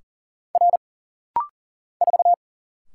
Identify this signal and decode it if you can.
Morse code, heard as a keyed tone, about 39 wpm, 700 Hz. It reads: R 4
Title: Single Letter-Number Alt - Rapid-Fire 40wpm